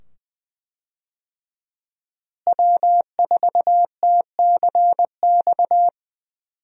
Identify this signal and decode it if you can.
Morse code, heard as a keyed tone, about 20 wpm, 700 Hz. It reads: W4TCX
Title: USA Call Signs - 5 Characters - Repeated 2x - 20wpm